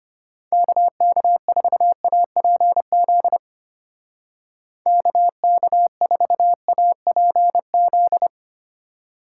Morse code, heard as a keyed tone, about 31 wpm, 700 Hz. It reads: KK4APZ KK4APZ